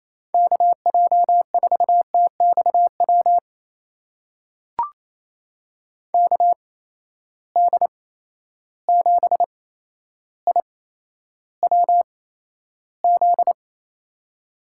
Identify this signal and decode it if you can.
Morse code, heard as a keyed tone, about 29 wpm, spaced as for 12 wpm, 700 Hz. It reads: KJ4TXW KD7IWZ
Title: USA Call Signs - 6 Characters - 8x Character Spacing - 28wpm